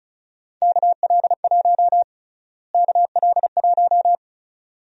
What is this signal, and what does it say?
Morse code, heard as a keyed tone, about 34 wpm, 700 Hz. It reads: KL1 KL1
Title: USA Call Sign Prefixes - Repeated 2x - 35wpm